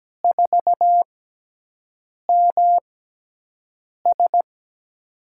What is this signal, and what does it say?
Morse code, heard as a keyed tone, about 17 wpm, spaced as for 6 wpm, 700 Hz. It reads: 4MS
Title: USA Call Signs - 5 Characters - 6x Character Spacing - 17wpm